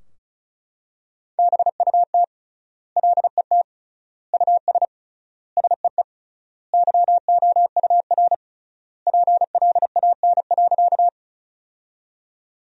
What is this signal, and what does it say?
Morse code, heard as a keyed tone, about 34 wpm, 700 Hz. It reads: BUT LET US SEE YOUR PLAN.
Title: 3x Word-Spacing - Sentences from Top 500 Words - 35wpm